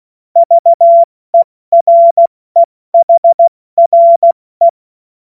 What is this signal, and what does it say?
Morse code, heard as a keyed tone, about 16 wpm, 680 Hz. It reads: VEREHRE